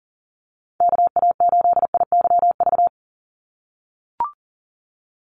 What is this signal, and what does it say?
Morse code, heard as a keyed tone, about 40 wpm, 700 Hz. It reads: KA8IYV
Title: USA Call Signs 40wpm